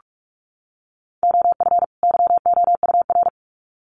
Morse code, heard as a keyed tone, about 45 wpm, 700 Hz. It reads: OF YOUR
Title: Sets of 2 Words 45wpm